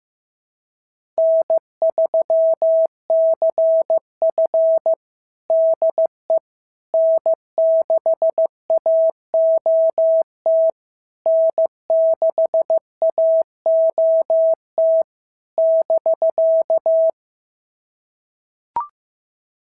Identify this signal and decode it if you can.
Morse code, heard as a keyed tone, about 15 wpm, 690 Hz. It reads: N3CF DE N6AOT N6AOT <BK>
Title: SOTA QSO Practice 15wpm